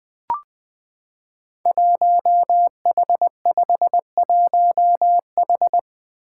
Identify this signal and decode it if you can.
Morse code, heard as a keyed tone, about 20 wpm, 700 Hz. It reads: 1H51H
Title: Five Challenging Letter-Number Combinations 20wpm